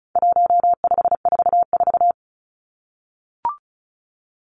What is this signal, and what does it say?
Morse code, heard as a keyed tone, about 34 wpm, 700 Hz. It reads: 1544
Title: Numbers - 4-Digit 35wpm